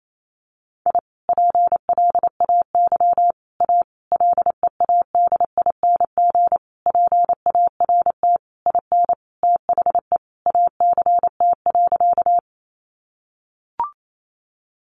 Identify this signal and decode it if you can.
Morse code, heard as a keyed tone, about 28 wpm, 700 Hz. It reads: I PLAY A LEADING PART IN THE ACT.
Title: Sentences from Top 1100 Words 28wpm